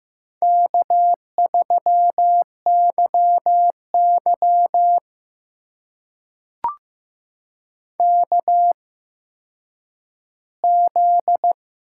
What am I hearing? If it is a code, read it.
Morse code, heard as a keyed tone, about 15 wpm, spaced as for 9 wpm, 700 Hz. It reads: K3YY KZ
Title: USA Call Signs - 4 Characters - 8x Character Spacing - 15wpm